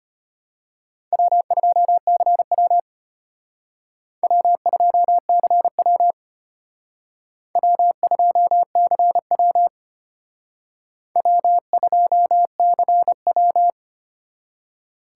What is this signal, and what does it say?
Morse code, heard as a keyed tone, about 32 wpm, 700 Hz. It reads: W2CW W2CW W2CW W2CW